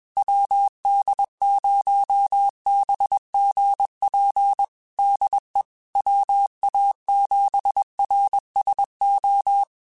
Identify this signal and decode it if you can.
Morse code, heard as a keyed tone, about 21 wpm, 800 Hz. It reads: WD0BGP DE WA7RSO
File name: QSOs / QSOs-mp3s / WD0BGP 1@20